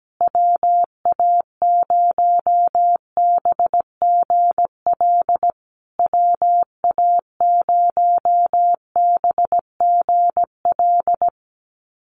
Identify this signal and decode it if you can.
Morse code, heard as a keyed tone, about 17 wpm, 700 Hz. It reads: WA0BGL WA0BGL
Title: USA Call Signs Repeated 3x 17wpm